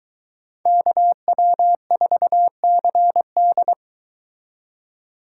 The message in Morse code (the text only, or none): KW4CD